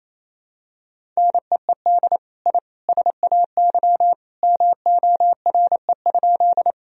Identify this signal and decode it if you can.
Morse code, heard as a keyed tone, about 28 wpm, 700 Hz. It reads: NEED I SAY MORE?
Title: Sentences from Top 1000 Words - Review Part 1 - 28wpm